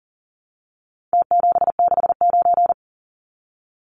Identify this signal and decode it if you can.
Morse code, heard as a keyed tone, about 40 wpm, 700 Hz. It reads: T769
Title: Numbers - Contest Style - 4-Digit 40wpm